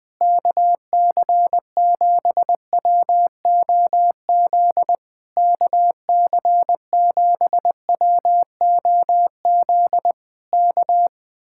Morse code, heard as a keyed tone, about 20 wpm, 700 Hz. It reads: KC7WOZ KC7WOZ K
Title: Speed-Racing Common QSO Phrases Encore - 20wpm